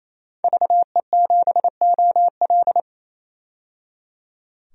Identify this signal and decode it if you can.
Morse code, heard as a keyed tone, about 27 wpm, 700 Hz. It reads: VE7OL
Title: DXCC-100 Call Signs - 5 Characters - 28wpm